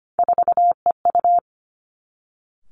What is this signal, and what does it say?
Morse code, heard as a keyed tone, about 25 wpm, 700 Hz. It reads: 4EU
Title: Three Challenging Letter-Number Combinations 25wpm